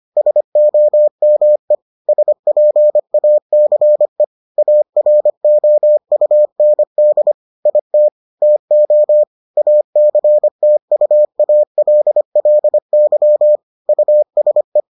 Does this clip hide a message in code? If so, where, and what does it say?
Yes: Morse code from 0.2 to 14.8 s, heard as a keyed tone, about 25 wpm, 600 Hz. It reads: SOME SPACE AROUND IT TO ACTUALLY USE